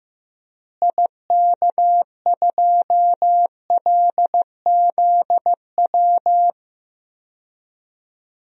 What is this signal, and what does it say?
Morse code, heard as a keyed tone, about 15 wpm, 700 Hz. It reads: IK2LZW
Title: DXCC-100 Call Signs - 6 Characters - Repeated 2x - 15wpm